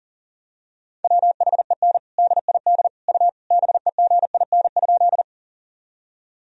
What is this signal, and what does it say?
Morse code, heard as a keyed tone, about 37 wpm, 700 Hz. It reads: WHEN DID U BEGIN?
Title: Sentences from Top 300 Words and Abbreviations 40wpm